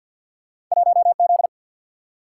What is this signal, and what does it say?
Morse code, heard as a keyed tone, about 54 wpm, 700 Hz. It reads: 1Z
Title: Single-Letter Single-Number 50wpm